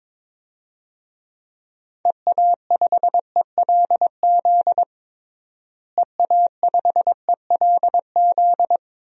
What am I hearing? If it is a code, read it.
Morse code, heard as a keyed tone, about 21 wpm, 700 Hz. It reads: EA5ELZ EA5ELZ
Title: DXCC-100 Call Signs - 6 Characters - Repeated 3x - 22wpm